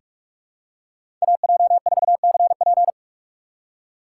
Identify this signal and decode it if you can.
Morse code, heard as a keyed tone, about 44 wpm, 700 Hz. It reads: AJ4CP
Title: USA Call Signs - 3x Character Spacing - 45wpm